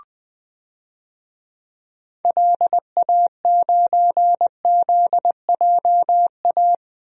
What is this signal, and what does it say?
Morse code, heard as a keyed tone, about 21 wpm, 700 Hz. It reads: LA9ZJA
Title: DXCC-100 Call Signs - 6 Characters - 20wpm